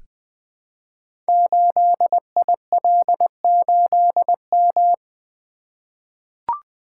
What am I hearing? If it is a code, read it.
Morse code, heard as a keyed tone, about 20 wpm, 700 Hz. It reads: 8IL8M
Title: Code Groups - 5 Letter:Numbers - 20wpm